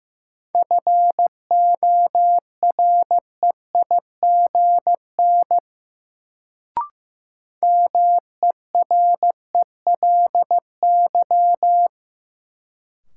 Morse code, heard as a keyed tone, about 15 wpm, 700 Hz. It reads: FOREIGN MERELY
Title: Sentences from Top 1400 Words - New Word Drill - 15wpm